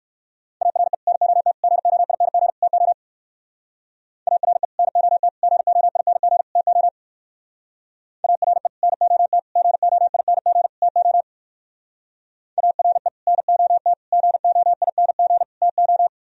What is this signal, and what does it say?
Morse code, heard as a keyed tone, about 64 wpm, 700 Hz. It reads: ARE NOT GOING TO ARE NOT GOING TO ARE NOT GOING TO ARE NOT GOING TO